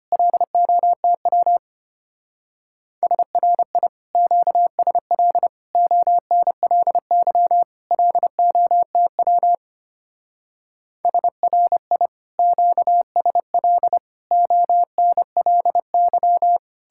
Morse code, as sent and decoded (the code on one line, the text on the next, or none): .-.. --- - .-- / ... .-. .. / --.- ... .-.. / --- -. .-.. -.-- / .-.. --- - .-- / ... .-. .. / --.- ... .-.. / --- -. .-.. -.--
LOTW SRI QSL ONLY LOTW SRI QSL ONLY